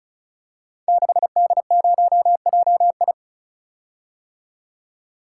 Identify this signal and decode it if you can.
Morse code, heard as a keyed tone, about 35 wpm, 700 Hz. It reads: 6D0JI
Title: Code Groups - 5 Letter:Numbers - 35wpm